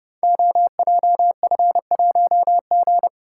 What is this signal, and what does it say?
Morse code, heard as a keyed tone, about 30 wpm, 700 Hz. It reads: OJF1G
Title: Code Groups - 5 Letter:Numbers - 30wpm